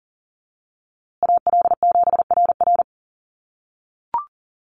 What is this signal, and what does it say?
Morse code, heard as a keyed tone, about 40 wpm, 700 Hz. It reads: AL7RR
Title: USA Call Signs Repeated 3x 40wpm